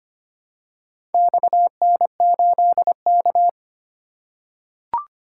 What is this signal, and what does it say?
Morse code, heard as a keyed tone, about 25 wpm, 700 Hz. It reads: XN8K
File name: practice / 25wpm / Four Letter-Number Combinations - 25wpm